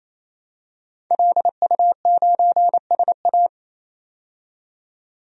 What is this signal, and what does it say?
Morse code, heard as a keyed tone, about 28 wpm, 700 Hz. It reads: LU9SA
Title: DXCC-100 Call Signs - 5 Characters - 28wpm